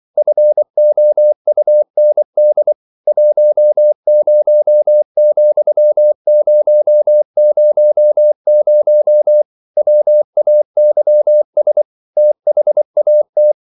Morse code, heard as a keyed tone, about 24 wpm, 600 Hz. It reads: FOUND 10,000 WAYS THAT
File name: CW / Practice / EnglishQuotations0224